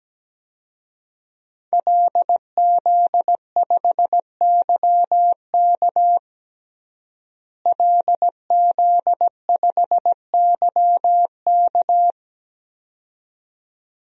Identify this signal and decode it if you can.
Morse code, heard as a keyed tone, about 17 wpm, 700 Hz. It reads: LZ5YK LZ5YK
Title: DXCC-100 Call Signs - 5 Characters - Repeated 2x - 17wpm